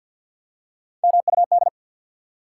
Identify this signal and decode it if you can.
Morse code, heard as a keyed tone, about 53 wpm, 700 Hz. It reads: MUD
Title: Three Letter Words - 50wpm